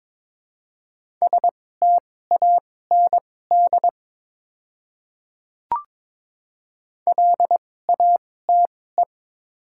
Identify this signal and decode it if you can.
Morse code, heard as a keyed tone, about 22 wpm, spaced as for 16 wpm, 700 Hz. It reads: STAND LATE
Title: 2x Character Spacing - Top 300 Words - 22wpm